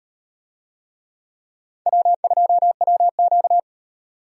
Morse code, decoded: W2WQ

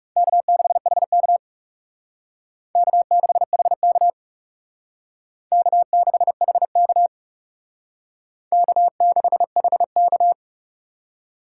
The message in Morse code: K6HK K6HK K6HK K6HK

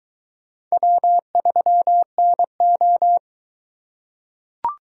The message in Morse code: W3NO

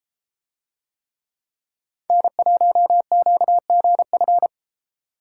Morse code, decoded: N1QGF

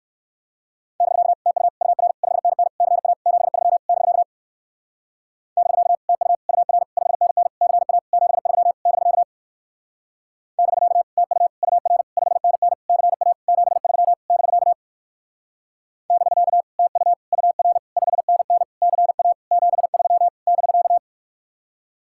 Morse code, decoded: <BK> TU UR 5NN CA 73 <BK> <BK> TU UR 5NN CA 73 <BK> <BK> TU UR 5NN CA 73 <BK> <BK> TU UR 5NN CA 73 <BK>

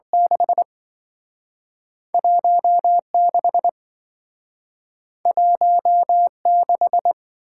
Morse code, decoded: 6 16 16

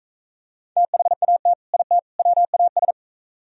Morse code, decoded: THAT IT WAS

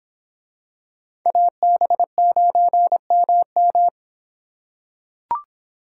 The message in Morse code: AB9MM

